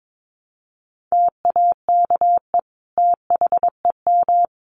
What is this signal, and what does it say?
Morse code, heard as a keyed tone, about 22 wpm, 700 Hz. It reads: TAKE THEM
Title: Sentences from Top 200 Words 22wpm